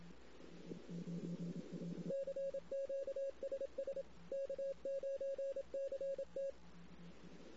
Morse code, heard as a keyed tone, about 27 wpm, 550 Hz. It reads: CQ SS K9CT
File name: CW / 2008-ss-cw-k5tr-op-n5rz -16:00